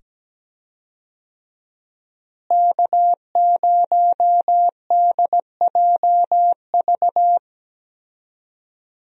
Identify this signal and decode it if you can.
Morse code, heard as a keyed tone, about 17 wpm, 700 Hz. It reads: K0DJV
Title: USA Call Signs - 5 Characters - Repeated 2x - 17wpm